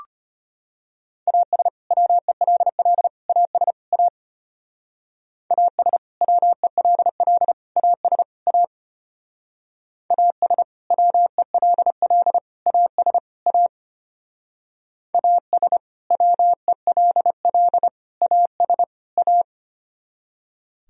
Morse code, heard as a keyed tone, about 29 wpm, 700 Hz. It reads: AS WELL AS A AS WELL AS A AS WELL AS A AS WELL AS A
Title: Speed-Racing - Sets of 4 Words 25wpm